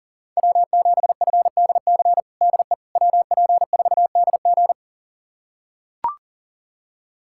Morse code, heard as a keyed tone, about 40 wpm, 700 Hz. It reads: W7FDC DE WP4DG
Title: USA Callsign DE Callsign - 5 Characters - 40wpm